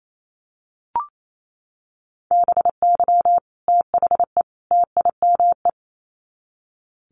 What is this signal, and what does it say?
Morse code, heard as a keyed tone, about 28 wpm, 700 Hz. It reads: BY THE TIME
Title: Sets of 3 Words 28wpm